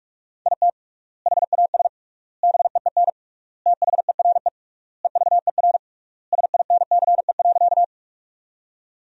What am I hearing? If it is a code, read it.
Morse code, heard as a keyed tone, about 46 wpm, 700 Hz. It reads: IT HAS BEEN THERE EVER SINCE.
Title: 3x Word-Spacing - Sentences from Top 400 Words - 45wpm